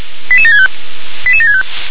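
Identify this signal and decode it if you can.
Five-tone selective call (ZVEI), heard as quick runs of five tones, tones of about 70 ms. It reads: 88765 88765
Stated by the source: ZVEI five-tone call 88775, received on an SDR.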